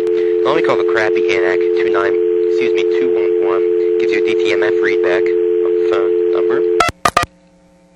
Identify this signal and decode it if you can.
Dial tone (North America), 350 + 440 Hz, with speech.
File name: TurtleCreekEWSD